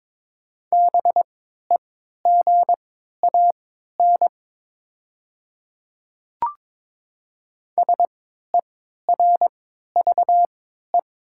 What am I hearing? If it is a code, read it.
Morse code, heard as a keyed tone, about 22 wpm, spaced as for 12 wpm, 700 Hz. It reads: BEGAN SERVE
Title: 3x Character Spacing - Top 500 Words - 22wpm